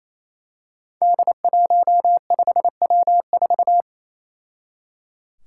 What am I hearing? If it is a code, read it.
Morse code, heard as a keyed tone, about 29 wpm, 700 Hz. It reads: D15W4
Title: Five Challenging Letter-Number Combinations 28wpm